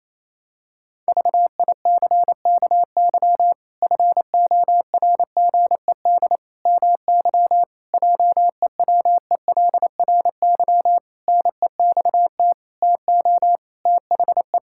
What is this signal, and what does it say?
Morse code, heard as a keyed tone, about 28 wpm, 700 Hz. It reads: VICKY FORGED MY JEWELRY NEXT TO THE
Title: Pangram Challenge v2 - 28wpm